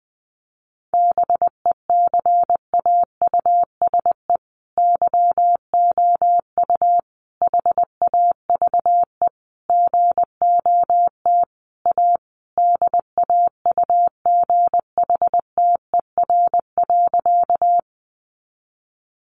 Morse code, heard as a keyed tone, about 20 wpm, 700 Hz. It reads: BECAUSE YOU HAVE GOT A DAUGHTER.
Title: Sentences from Top 1500 Words 20wpm